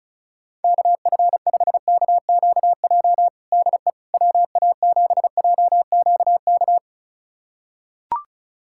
Morse code, heard as a keyed tone, about 34 wpm, 700 Hz. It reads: KF5KQJ DE WA7JQK
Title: USA Callsign DE Callsign 35wpm